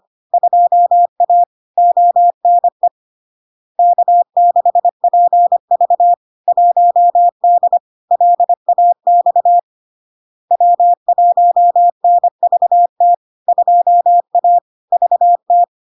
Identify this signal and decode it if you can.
Morse code, heard as a keyed tone, about 25 wpm, 700 Hz. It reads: 2A ONE K6PV 1D LAX W1NVT 2A VT
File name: Log-in-CW-25wpm-700-Hz